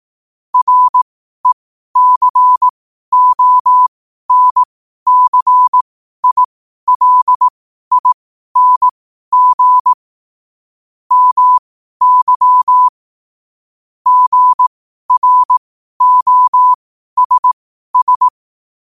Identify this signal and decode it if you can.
Morse code, heard as a keyed tone, about 18 wpm, spaced as for 12 wpm, 1000 Hz. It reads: RECONCILING MY GROSS
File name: 12wpm